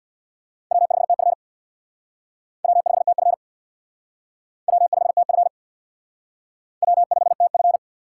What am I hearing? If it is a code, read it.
Morse code, heard as a keyed tone, about 62 wpm, 700 Hz. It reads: W5TF W5TF W5TF W5TF
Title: Speed Racing - USA Call Signs 50wpm